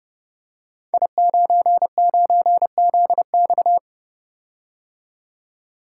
Morse code, heard as a keyed tone, about 30 wpm, 700 Hz. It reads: I99ZX
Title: Code Groups - 5 Letter:Numbers - 30wpm